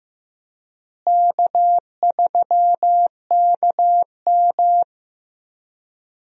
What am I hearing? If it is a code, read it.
Morse code, heard as a keyed tone, about 16 wpm, 700 Hz. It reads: K3KM
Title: USA Call Signs - 4 Characters - Repeated 3x - 15wpm